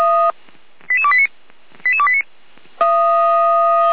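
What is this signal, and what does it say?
Five-tone selective call (ZVEI), heard as quick runs of five tones, tones of about 80 ms. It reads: 88289 88289
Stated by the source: ZVEI five-tone call 88267 received on SDR.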